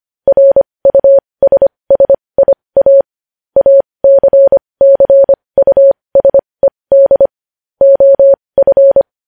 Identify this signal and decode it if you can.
Morse code, heard as a keyed tone, about 25 wpm, spaced as for 20 wpm, 550 Hz. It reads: RUSSIA ACCUSED OF
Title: News Headlines at 20